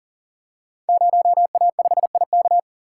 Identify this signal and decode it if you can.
Morse code, heard as a keyed tone, about 39 wpm, 700 Hz. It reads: 0A5IK
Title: Code Groups - 5 Letter:Numbers - 40wpm